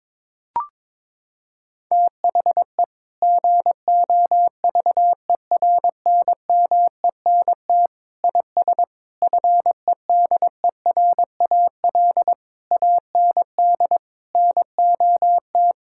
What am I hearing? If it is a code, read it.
Morse code, heard as a keyed tone, about 22 wpm, 700 Hz. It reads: THE GOVERNMENT IS FEDERAL AND NOT
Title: Sentences from Top 1000 Words - Review Part 5 - 22wpm